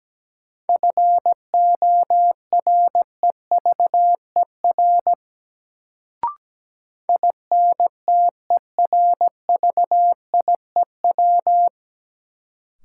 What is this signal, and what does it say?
Morse code, heard as a keyed tone, about 17 wpm, 700 Hz. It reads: FOREVER INTERVIEW
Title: Sentences from Top 1600 Words - New Word Drill Alt - 17wpm